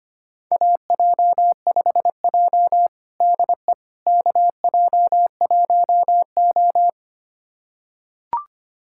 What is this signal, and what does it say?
Morse code, heard as a keyed tone, about 25 wpm, 700 Hz. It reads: AJ5J DE KJ1O
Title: USA Callsign DE Callsign - 4 Characters - 25wpm